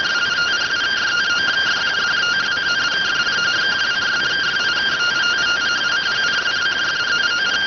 Two-tone teleprinter signal, about 170 Hz shift.